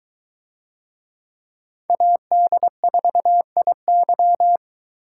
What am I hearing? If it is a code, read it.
Morse code, heard as a keyed tone, about 22 wpm, 700 Hz. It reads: AD4IY